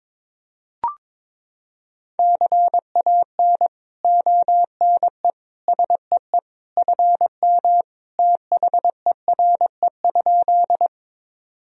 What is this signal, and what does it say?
Morse code, heard as a keyed tone, about 22 wpm, 700 Hz. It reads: CAN ONE SEE FM THERE?
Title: Sentences from Top 100 Words and Abbreviations 22wpm